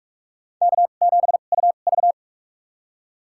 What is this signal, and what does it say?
Morse code, heard as a keyed tone, about 47 wpm, spaced as for 33 wpm, 700 Hz. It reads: K7UV
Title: USA Call Signs - 2x Character Spacing - 45wpm